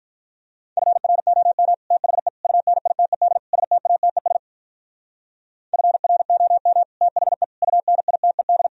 Morse code, heard as a keyed tone, about 45 wpm, 700 Hz. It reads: FROM THE UNITED STATES FROM THE UNITED